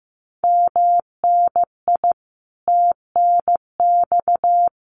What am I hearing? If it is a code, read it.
Morse code, heard as a keyed tone, about 14 wpm, 700 Hz. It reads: MNI TNX
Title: Common QSO Phrases Encore 15wpm